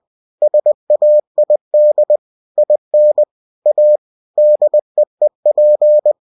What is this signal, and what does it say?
Morse code, heard as a keyed tone, about 20 wpm, 600 Hz. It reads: SAID IN A DEEP
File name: cw-books / winnie-the-pooh / winnie-20wpm-page-0005